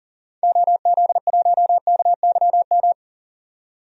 Morse code, heard as a keyed tone, about 40 wpm, 700 Hz. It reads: OZ1KYM
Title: DXCC-100 Call Signs - 6 Characters - 40wpm